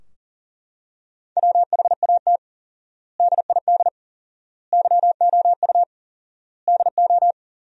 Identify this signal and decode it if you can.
Morse code, heard as a keyed tone, about 40 wpm, 700 Hz. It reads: WHAT DID YOU DO
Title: 4x Word-Spacing - Sentences from Top 100 Words - 40wpm